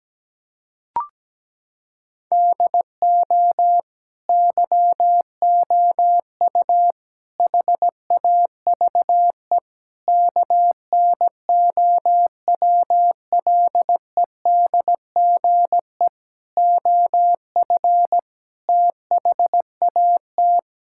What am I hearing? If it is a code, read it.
Morse code, heard as a keyed tone, about 17 wpm, 700 Hz. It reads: DO YOU HAVE KNOWLEDGE OF THAT
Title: Sentences from Top 1000 Words - Review Part 6 - 17wpm